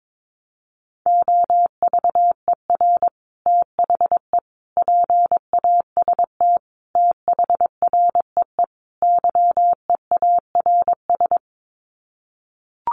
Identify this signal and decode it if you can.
Morse code, heard as a keyed tone, about 22 wpm, 700 Hz. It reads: OVER THE PAST THREE YEARS
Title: Sets of 5 Words Encore2 22wpm